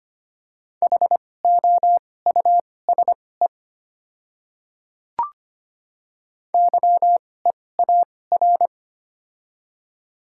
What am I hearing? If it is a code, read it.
Morse code, heard as a keyed tone, about 25 wpm, spaced as for 18 wpm, 700 Hz. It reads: HOUSE YEAR
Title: 2x Character Spacing - Top 500 Words - 25wpm